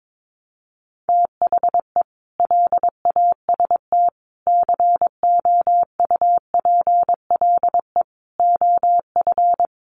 Morse code, heard as a keyed tone, about 22 wpm, 700 Hz. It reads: THE LAST COUPLE OF
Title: Sets of 5 Words Encore 22wpm